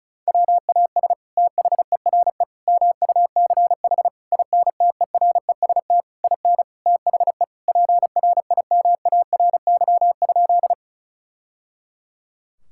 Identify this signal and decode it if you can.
Morse code, heard as a keyed tone, about 34 wpm, 700 Hz. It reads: WAS THERE MUCH INTEREST IN THE PRIMARY?
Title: Sentences from Top 1000 Words - Review Part 5 - 35wpm